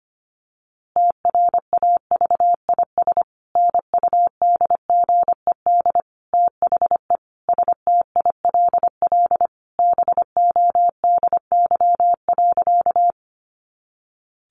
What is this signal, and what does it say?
Morse code, heard as a keyed tone, about 25 wpm, 700 Hz. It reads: TRAVIS NUDGED THE STILL BODY.
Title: A Sound Of Thunder 25wpm